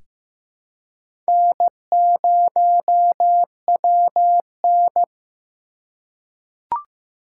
Morse code, heard as a keyed tone, about 15 wpm, 700 Hz. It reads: N0WN
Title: USA Call Signs - 4 Characters - 2x Character Spacing - 15wpm